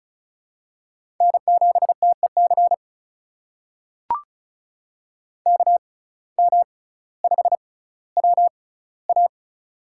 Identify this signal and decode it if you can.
Morse code, heard as a keyed tone, about 34 wpm, spaced as for 18 wpm, 700 Hz. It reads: N7TEC KM5WA
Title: USA Call Signs - 5 Characters - 6x Character Spacing - 35wpm